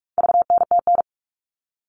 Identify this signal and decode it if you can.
Morse code, heard as a keyed tone, about 44 wpm, 700 Hz. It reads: VNTN
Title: All Cut Numbers - 4-Digit 45wpm